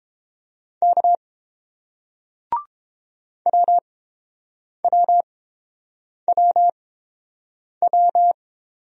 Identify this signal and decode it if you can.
Morse code, heard as a keyed tone, about 30 wpm, spaced as for 7 wpm, 700 Hz. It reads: K WWWW